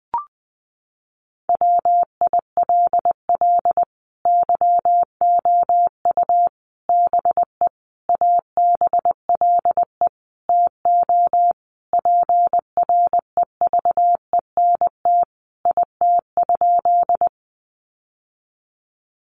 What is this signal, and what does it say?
Morse code, heard as a keyed tone, about 20 wpm, 700 Hz. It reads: WILL YOU BE ABLE TO PREVENT IT?
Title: Sentences from Top 1100 Words 20wpm